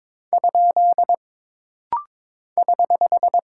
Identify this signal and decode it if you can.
Morse code, heard as a keyed tone, about 22 wpm, 700 Hz. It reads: ? <HH>
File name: practice / 22wpm / Single Punctuation-Prosign - Rapid-Fire - 22wpm